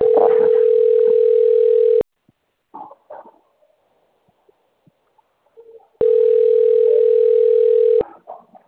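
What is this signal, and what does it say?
Ringback tone (North America), 440 + 480 Hz.
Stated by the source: Outbound phone call recorded on a PBX system labeled Carioca 3.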